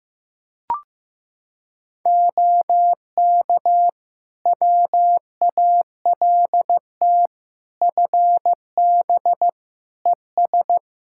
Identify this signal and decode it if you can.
Morse code, heard as a keyed tone, about 15 wpm, 700 Hz. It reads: OK WALT FB ES
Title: Practice QSO - Exchange 2 - Info - 15wpm